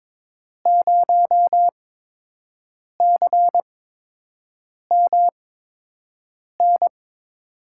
Morse code, heard as a keyed tone, about 22 wpm, spaced as for 6 wpm, 700 Hz. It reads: 0CMN
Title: USA Call Signs - 6 Characters - 8x Character Spacing - 22wpm